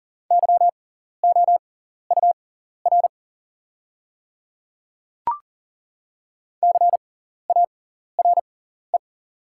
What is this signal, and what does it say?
Morse code, heard as a keyed tone, about 40 wpm, spaced as for 14 wpm, 700 Hz. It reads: YOUR CARE
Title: 6x Character Spacing - Top 300 Words - 40wpm